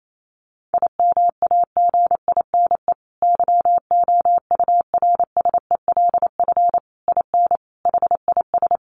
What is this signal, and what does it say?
Morse code, heard as a keyed tone, about 28 wpm, 700 Hz. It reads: IMAGINE YOURSELF IN HIS
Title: Sentences from Top 1400 Words 28wpm